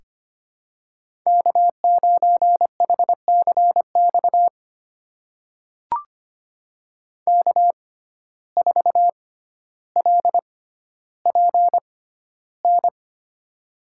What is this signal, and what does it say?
Morse code, heard as a keyed tone, about 25 wpm, spaced as for 13 wpm, 700 Hz. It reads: K9HCX K4LPN
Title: USA Call Signs - 5 Characters - 6x Character Spacing - 25wpm